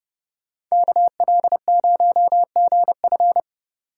Morse code, heard as a keyed tone, about 36 wpm, 700 Hz. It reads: KL0GF
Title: Speed Racing - USA 2x2 Call Signs 30wpm